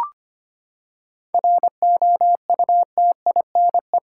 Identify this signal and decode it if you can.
Morse code, heard as a keyed tone, about 25 wpm, 700 Hz. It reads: ROUTINE
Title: Seven Letter Words - 25wpm